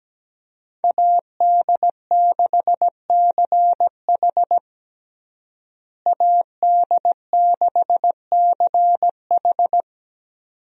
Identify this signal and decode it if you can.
Morse code, heard as a keyed tone, about 17 wpm, 700 Hz. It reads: AD6CH AD6CH
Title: USA Call Signs - 5 Characters - Repeated 2x - 17wpm